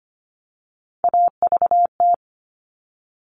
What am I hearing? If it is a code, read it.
Morse code, heard as a keyed tone, about 25 wpm, 700 Hz. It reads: AVT
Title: All Cut Numbers - 3-Digit 25wpm